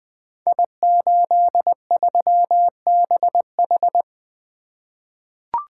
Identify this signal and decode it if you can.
Morse code, heard as a keyed tone, about 20 wpm, 700 Hz. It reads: I83BH
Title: Code Groups - 5 Letter:Numbers - 20wpm